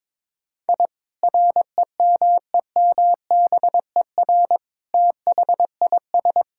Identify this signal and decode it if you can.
Morse code, heard as a keyed tone, about 22 wpm, 700 Hz. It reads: I REMEMBER THIS
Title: Sentences from Top 1000 Words - Review Part 5 - 22wpm